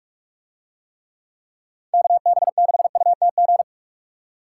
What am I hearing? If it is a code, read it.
Morse code, heard as a keyed tone, about 41 wpm, 700 Hz. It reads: KB6UTG